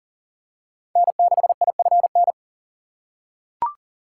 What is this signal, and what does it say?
Morse code, heard as a keyed tone, about 40 wpm, 700 Hz. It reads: N6IFN
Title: USA Call Signs - 3x Character Spacing - 40wpm